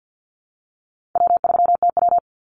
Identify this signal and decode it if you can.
Morse code, heard as a keyed tone, about 50 wpm, 700 Hz. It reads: W3TW
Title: Easy Call Signs Repeated 3x - 50wpm